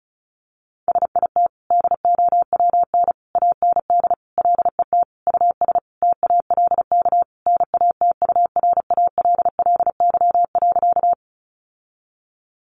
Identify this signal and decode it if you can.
Morse code, heard as a keyed tone, about 34 wpm, 700 Hz. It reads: SIT DOWN AND LET US TALK NATURALLY.
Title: Sentences from Top 1300 Words 35wpm